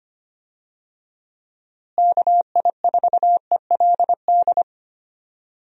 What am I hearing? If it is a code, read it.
Morse code, heard as a keyed tone, about 25 wpm, 700 Hz. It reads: KI4ELD
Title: USA Call Signs - 6 Characters - Repeated 3x - 25wpm